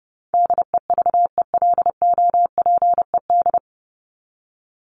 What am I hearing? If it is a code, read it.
Morse code, heard as a keyed tone, about 30 wpm, 700 Hz. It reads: DEVELOPED
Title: ED Suffix - Top 100 Words with Base Words 30wpm